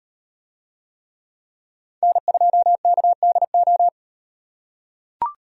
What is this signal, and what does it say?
Morse code, heard as a keyed tone, about 35 wpm, 700 Hz. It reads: N2KDO